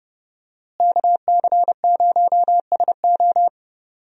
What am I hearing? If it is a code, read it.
Morse code, heard as a keyed tone, about 30 wpm, 700 Hz. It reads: KC0SO
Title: USA Call Signs - 5 Characters - Repeated 2x - 30wpm